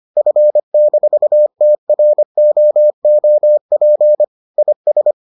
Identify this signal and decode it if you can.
Morse code, heard as a keyed tone, about 25 wpm, 600 Hz. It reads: F-TROOP IS